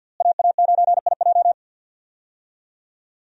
Morse code, heard as a keyed tone, about 53 wpm, 700 Hz. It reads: AA9IJ
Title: USA Call Signs - 5 Characters - Repeated 2x - 50wpm